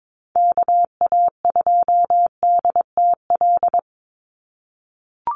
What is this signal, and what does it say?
Morse code, heard as a keyed tone, about 22 wpm, 700 Hz. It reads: KA2DTL
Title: USA Call Signs 22wpm